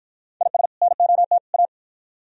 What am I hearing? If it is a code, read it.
Morse code, heard as a keyed tone, about 52 wpm, 700 Hz. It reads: IS NOT A